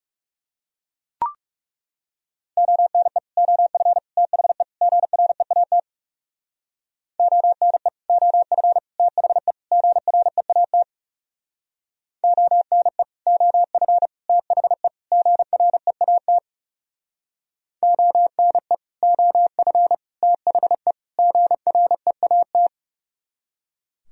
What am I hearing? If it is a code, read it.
Morse code, heard as a keyed tone, about 37 wpm, 700 Hz. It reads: ONE OF THE GREAT ONE OF THE GREAT ONE OF THE GREAT ONE OF THE GREAT
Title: Speed-Racing - Sets of 4 Words 30wpm